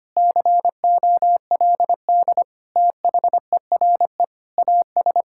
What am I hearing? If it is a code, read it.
Morse code, heard as a keyed tone, about 25 wpm, 700 Hz. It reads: COLD THERE AS
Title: Sentences from Top 1600 Words 25wpm